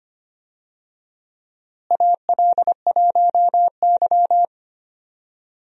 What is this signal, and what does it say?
Morse code, heard as a keyed tone, about 25 wpm, 700 Hz. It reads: AL1Y
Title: USA Call Signs - 4 Characters - Repeated 3x - 25wpm